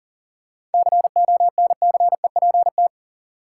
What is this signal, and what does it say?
Morse code, heard as a keyed tone, about 38 wpm, 700 Hz. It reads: CONCEPT
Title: Sentences from Top 1100 Words - New Word Drill - 40wpm